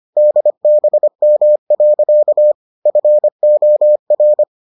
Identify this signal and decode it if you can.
Morse code, heard as a keyed tone, about 25 wpm, 600 Hz. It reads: DBM. FOR